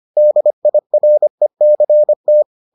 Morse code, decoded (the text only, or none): DIRECT